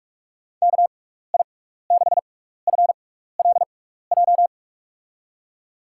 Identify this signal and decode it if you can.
Morse code, heard as a keyed tone, about 46 wpm, spaced as for 15 wpm, 700 Hz. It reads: KI6FLJ
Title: USA Call Signs - 6x Character Spacing - 45wpm